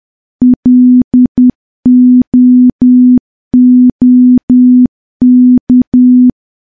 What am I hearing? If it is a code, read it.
Morse code, heard as a keyed tone, about 10 wpm, 250 Hz. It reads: LOOK
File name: morse / morse-training-10WPM